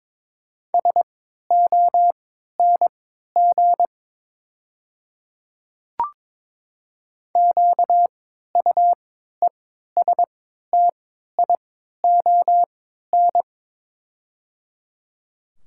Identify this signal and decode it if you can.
Morse code, heard as a keyed tone, about 21 wpm, spaced as for 12 wpm, 700 Hz. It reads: SONG QUESTION
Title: 3x Character Spacing - Top 500 Words - 22wpm